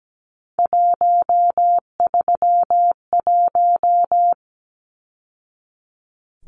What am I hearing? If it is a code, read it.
Morse code, heard as a keyed tone, about 17 wpm, 700 Hz. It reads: 131
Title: Numbers - 3-Digit 17wpm